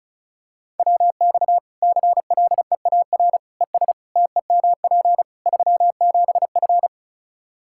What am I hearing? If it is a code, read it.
Morse code, heard as a keyed tone, about 34 wpm, 700 Hz. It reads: WX CLEAR ES TEMP 37F
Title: K1USN SST QSO Practice 35wpm